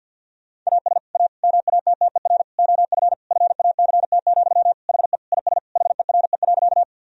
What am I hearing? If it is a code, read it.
Morse code, heard as a keyed tone, about 50 wpm, 700 Hz. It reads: AS A MATTER OF FACT, HE IS HERE.
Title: Sentences from Top 1000 Words - Review Part 3 - 50wpm